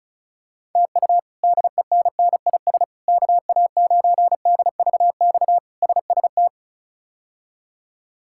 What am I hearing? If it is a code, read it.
Morse code, heard as a keyed tone, about 34 wpm, 700 Hz. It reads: TU DENNIS KA9DVX SST
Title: K1USN SST QSO Practice - 35wpm